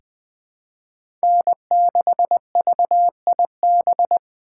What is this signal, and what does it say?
Morse code, heard as a keyed tone, about 20 wpm, 700 Hz. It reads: N6VIB
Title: USA Call Signs - 5 Characters - 2x Character Spacing - 20wpm